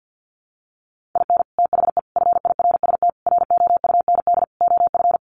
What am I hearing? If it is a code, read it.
Morse code, heard as a keyed tone, about 50 wpm, 700 Hz. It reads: IN THE FIRST ROUND OF
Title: Sets of 5 Words Encore 50wpm